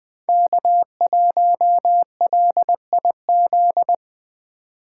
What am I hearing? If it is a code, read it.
Morse code, heard as a keyed tone, about 20 wpm, 700 Hz. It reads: K1LIZ
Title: 2023 SOTA Chaser Call Signs - 20wpm